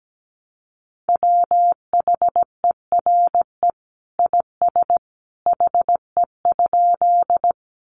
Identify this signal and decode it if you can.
Morse code, heard as a keyed tone, about 17 wpm, 700 Hz. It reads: WHERE IS HE?
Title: Sentences from Top 200 Words 17wpm